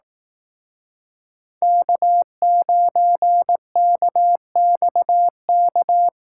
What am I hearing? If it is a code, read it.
Morse code, heard as a keyed tone, about 18 wpm, 700 Hz. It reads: K9KXK
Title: Speed Racing - USA 1x3 Call Signs 15wpm